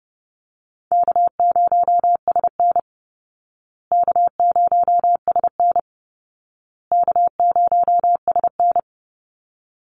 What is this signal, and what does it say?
Morse code, heard as a keyed tone, about 30 wpm, 700 Hz. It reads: K0SN K0SN K0SN
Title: Easy Call Signs Repeated 3x - 30wpm